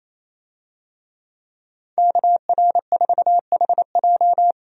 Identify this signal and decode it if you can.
Morse code, heard as a keyed tone, about 28 wpm, 700 Hz. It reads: KR4HJ
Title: USA Call Signs - 5 Characters - 28wpm